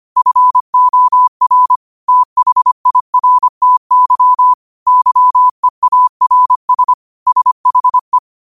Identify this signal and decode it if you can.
Morse code, heard as a keyed tone, about 25 wpm, 1000 Hz. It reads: FOR THIRTY YEARS SHE
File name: qotdfiles / 25wpm